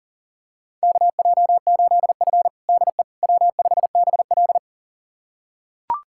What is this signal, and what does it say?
Morse code, heard as a keyed tone, about 40 wpm, 700 Hz. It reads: KJ8F DE W5BL
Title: USA Callsign DE Callsign - 4 Characters - 40wpm